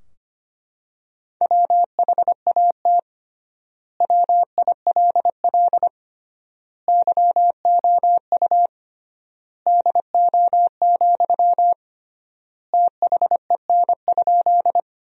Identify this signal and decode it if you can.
Morse code, heard as a keyed tone, about 25 wpm, 700 Hz. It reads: WHAT WILL YOU DO, THEN?
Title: 3x Word-Spacing - Sentences from Top 100 Words - 25wpm